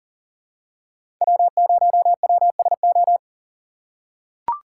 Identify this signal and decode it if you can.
Morse code, heard as a keyed tone, about 40 wpm, 700 Hz. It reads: W0WSO
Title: USA Call Signs - 5 Characters - 4x Character Spacing - 40wpm